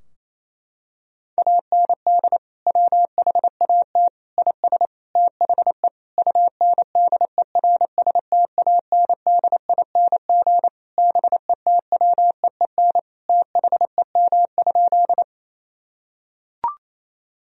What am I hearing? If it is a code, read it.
Morse code, heard as a keyed tone, about 28 wpm, 700 Hz. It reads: AND WHAT IS THE UNDERSTANDING BETWEEN THEM?
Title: Sentences from Top 1000 Words - Review Part 3 - 28wpm